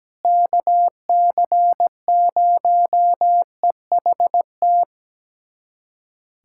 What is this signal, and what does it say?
Morse code, heard as a keyed tone, about 17 wpm, 700 Hz. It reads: KC0EHT
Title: USA Callsign DE Callsign 17wpm